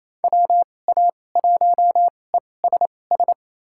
Morse code, heard as a keyed tone, about 28 wpm, spaced as for 20 wpm, 700 Hz. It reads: WA1ESS
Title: USA Call Signs - 6 Characters - 2x Character Spacing - 28wpm